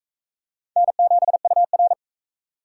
Morse code, heard as a keyed tone, about 39 wpm, 700 Hz. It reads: N7UR